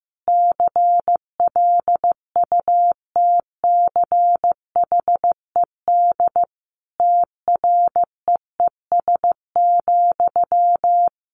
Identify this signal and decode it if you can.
Morse code, heard as a keyed tone, about 15 wpm, 700 Hz. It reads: CLUTCHED TREES,
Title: A Sound Of Thunder 15wpm